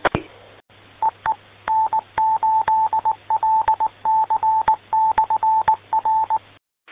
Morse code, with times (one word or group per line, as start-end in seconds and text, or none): 1.0-1.3 EE
1.7-6.4 N8LC/R